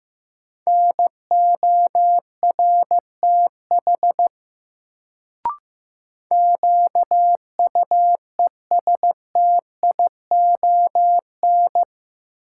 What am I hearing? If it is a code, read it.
Morse code, heard as a keyed tone, about 15 wpm, 700 Hz. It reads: NORTH QUESTION
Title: Sentences from Top 400 Words - New Word Drill - 15wpm